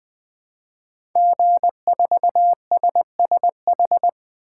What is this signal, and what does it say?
Morse code, heard as a keyed tone, about 20 wpm, 700 Hz. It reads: G4SSH
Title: 2023 SOTA Chaser Call Signs - 20wpm